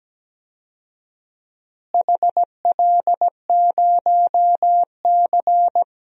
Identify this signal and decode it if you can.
Morse code, heard as a keyed tone, about 17 wpm, 700 Hz. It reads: HL0C
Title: DXCC-100 Call Signs - 4 Characters - Repeated 2x - 17wpm